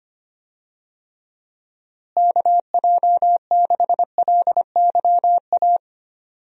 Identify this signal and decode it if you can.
Morse code, heard as a keyed tone, about 25 wpm, 700 Hz. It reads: KJ6LYA
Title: USA Call Signs - 6 Characters - Repeated 2x - 25wpm